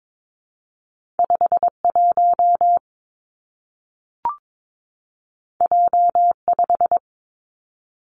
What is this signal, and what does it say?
Morse code, heard as a keyed tone, about 22 wpm, 700 Hz. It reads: 5 1 J 5
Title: Two Challenging Letter-Number Combinations 22wpm